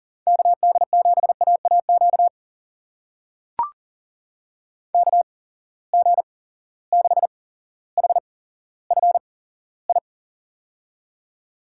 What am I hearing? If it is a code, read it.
Morse code, heard as a keyed tone, about 40 wpm, spaced as for 17 wpm, 700 Hz. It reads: KD7AAQ KG6HFI
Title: USA Call Signs - 6 Characters - 8x Character Spacing - 40wpm